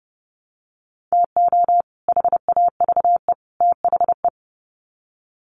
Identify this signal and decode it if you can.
Morse code, heard as a keyed tone, about 28 wpm, 700 Hz. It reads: TO HAVE THE
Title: Sets of 3 Words 30wpm